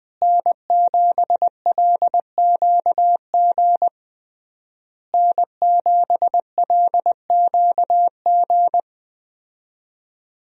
Morse code, heard as a keyed tone, about 20 wpm, 700 Hz. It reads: N7LQG N7LQG
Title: USA Call Signs - 5 Characters - Repeated 3x - 20wpm